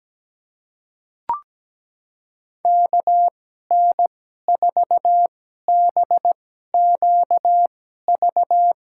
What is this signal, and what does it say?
Morse code, heard as a keyed tone, about 17 wpm, spaced as for 12 wpm, 700 Hz. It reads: KN4BQV
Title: USA Call Signs - 2x Character Spacing - 17wpm